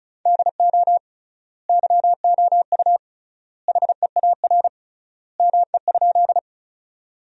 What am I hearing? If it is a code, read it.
Morse code, heard as a keyed tone, about 36 wpm, 700 Hz. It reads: DO YOU HEAR ME?
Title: 3x Word-Spacing - Sentences from Top 300 Words - 35wpm